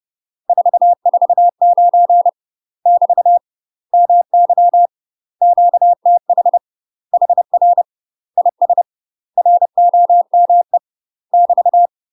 Morse code, heard as a keyed tone, about 30 wpm, 700 Hz. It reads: 449 <BT> MY QTH HR IS ROME <BT>
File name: UK MASTER QSOs TRAINING / 30WPM / UK MASTER QSOs 0043